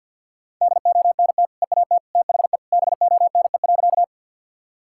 Morse code, heard as a keyed tone, about 50 wpm, 700 Hz. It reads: DONT EAT THE BONE.
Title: Sentences from Top 1600 Words 50wpm